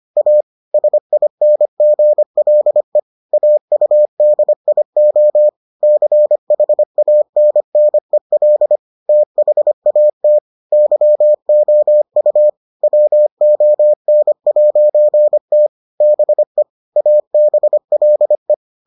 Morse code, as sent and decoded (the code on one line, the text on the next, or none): .- / ... .. -. --. .-.. / .- ..- -.. .. --- / -.-. .... .- -. -. .-.. / - .... .- - / -.-- --- ..- / .-- --- -. .----. - / -... / .- -... .-.. .
A SINGLE AUDIO CHANNEL THAT YOU WON'T BE ABLE